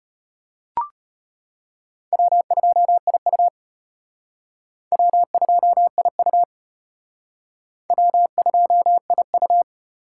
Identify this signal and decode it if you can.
Morse code, heard as a keyed tone, about 33 wpm, 700 Hz. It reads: W2IU W2IU W2IU